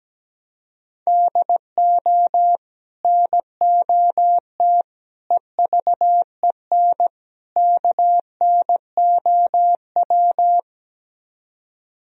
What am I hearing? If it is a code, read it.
Morse code, heard as a keyed tone, about 17 wpm, 700 Hz. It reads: DO NOT EVEN KNOW
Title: Speed-Racing - Sets of 4 Words 17wpm